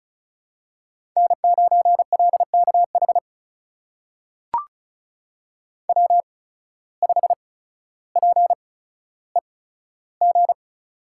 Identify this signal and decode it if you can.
Morse code, heard as a keyed tone, about 36 wpm, spaced as for 15 wpm, 700 Hz. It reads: N9LKH W5PEG
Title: USA Call Signs - 5 Characters - 8x Character Spacing - 35wpm